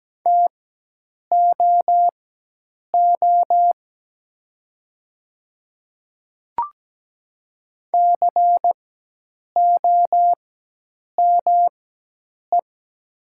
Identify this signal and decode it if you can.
Morse code, heard as a keyed tone, about 17 wpm, spaced as for 8 wpm, 700 Hz. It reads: TOO COME
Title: 4x Character Spacing - Top 300 Words - 17wpm